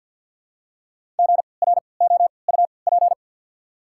Morse code, heard as a keyed tone, about 53 wpm, spaced as for 29 wpm, 690 Hz. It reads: GROUP
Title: 3x Character Spacing - Top 300 Words - 50wpm